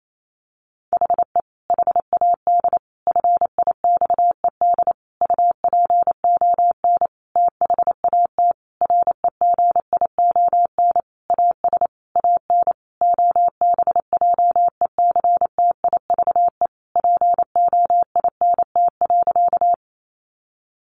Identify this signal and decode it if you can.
Morse code, heard as a keyed tone, about 28 wpm, 700 Hz. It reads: HE HAD FIXED UPON THAT REGION AS AN OBJECTIVE POINT.
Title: Sentences from Top 1500 Words 28wpm